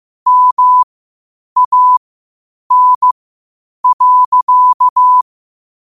Morse code, heard as a keyed tone, about 15 wpm, spaced as for 8 wpm, 1000 Hz. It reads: MAN.
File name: qotdfiles / 8wpm